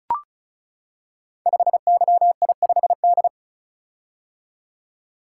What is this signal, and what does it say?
Morse code, heard as a keyed tone, about 36 wpm, 690 Hz. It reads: 5YI5D
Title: Code Groups - 5 Letter:Numbers - 35wpm